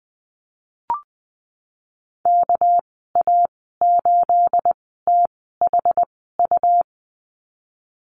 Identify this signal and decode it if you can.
Morse code, heard as a keyed tone, about 20 wpm, spaced as for 15 wpm, 700 Hz. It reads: KA8THU
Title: 231 Lesson 37g - Letter x - Call Signs 20wpm